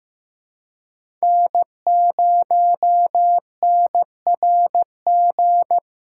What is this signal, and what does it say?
Morse code, heard as a keyed tone, about 15 wpm, 700 Hz. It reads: N0NRG
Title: USA Callsign DE Callsign 15wpm